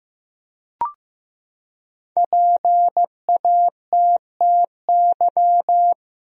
Morse code, heard as a keyed tone, about 15 wpm, 700 Hz. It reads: PATTY
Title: USA Common Names - Max 5 Letters - 15wpm